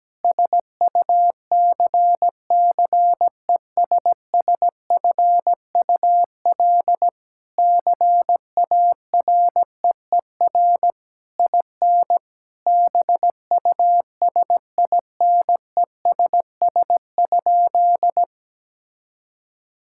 Morse code, heard as a keyed tone, about 17 wpm, 700 Hz. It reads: SUCCESSFUL CAREER IN BUSINESS?
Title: Sentences from Top 1000 Words - Review Part 6 - 17wpm